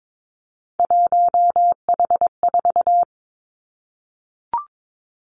Morse code, heard as a keyed tone, about 22 wpm, 700 Hz. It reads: 1H4
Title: Three Challenging Letter-Number Combinations 22wpm